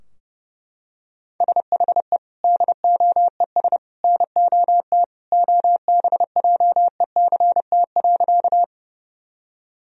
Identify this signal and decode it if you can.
Morse code, heard as a keyed tone, about 30 wpm, 700 Hz. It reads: SHE DOES NOT OBJECT.
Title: Sentences from Top 1000 Words - Review Part 3 - 30wpm